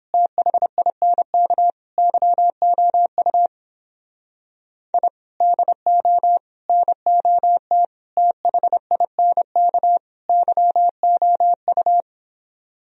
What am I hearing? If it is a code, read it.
Morse code, heard as a keyed tone, about 26 wpm, 700 Hz. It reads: THINK YOU I DO NOT THINK YOU
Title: Speed-Racing - Sets of 5 Words 22wpm